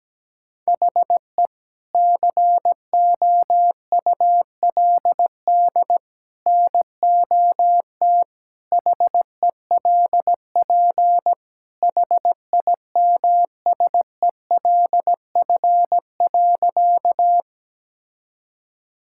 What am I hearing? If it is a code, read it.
Morse code, heard as a keyed tone, about 17 wpm, 700 Hz. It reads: HE COULD NOT HELP HIMSELF.
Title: Sentences from Top 1600 Words 17wpm